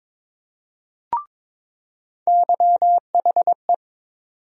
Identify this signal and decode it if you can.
Morse code, heard as a keyed tone, about 22 wpm, 700 Hz. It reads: YHE
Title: USA Call Sign Postfixes - 3 Letters 22wpm